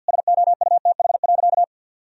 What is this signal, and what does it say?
Morse code, heard as a keyed tone, about 50 wpm, 700 Hz. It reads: SOUTH.
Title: Sentences from Top 400 Words and Abbreviations 50wpm